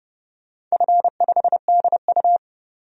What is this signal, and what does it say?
Morse code, heard as a keyed tone, about 31 wpm, 700 Hz. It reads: F5DU
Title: DXCC-100 Call Signs - 4 Characters - 30wpm